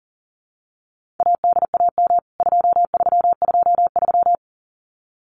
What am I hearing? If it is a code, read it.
Morse code, heard as a keyed tone, about 40 wpm, 700 Hz. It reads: ADAM 2323
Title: CWT Exchange 40wpm